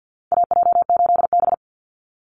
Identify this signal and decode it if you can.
Morse code, heard as a keyed tone, about 50 wpm, 700 Hz. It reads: AJ8B
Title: Call Signs Repeated 2x - 50wpm